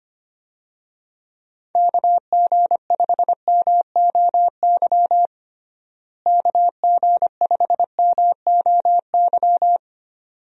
Morse code, heard as a keyed tone, about 25 wpm, 700 Hz. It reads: KG5MOY KG5MOY
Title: USA Call Signs - 6 Characters - Repeated 2x - 25wpm